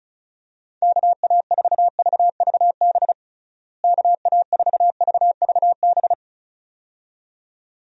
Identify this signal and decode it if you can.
Morse code, heard as a keyed tone, about 36 wpm, 700 Hz. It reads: KA4VVB KA4VVB
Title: USA Call Signs - 6 Characters - Repeated 2x - 35wpm